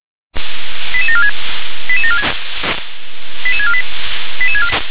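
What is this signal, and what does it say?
Five-tone selective call (ZVEI), heard as quick runs of five tones, tones of about 70 ms. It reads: 88746 88746 88648
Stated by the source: ZVEI five-tone selective call received on SDR.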